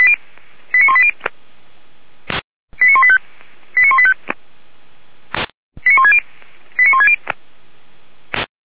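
Five-tone selective call (ZVEI), heard as quick runs of five tones, tones of about 70 ms. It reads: 89180 89185 89185 89160 89160
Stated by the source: ZVEI five-tone call 89132 received by SDR on channel 464, from the fire brigade server feuerwehr-orlamuende.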